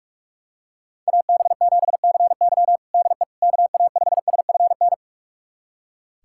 Morse code, heard as a keyed tone, about 43 wpm, 700 Hz. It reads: AB7CY DE KA5SFN
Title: USA Callsign DE Callsign 45wpm